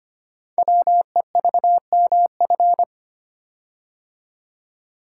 Morse code, heard as a keyed tone, about 25 wpm, 700 Hz. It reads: WEVMF
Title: Code Groups - 5 Letters - 25wpm